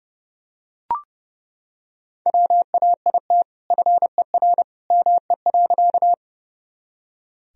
Morse code, heard as a keyed tone, about 30 wpm, 700 Hz. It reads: WAIT FER ME.
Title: Sentences from Top 500 Words and Abbreviations 30wpm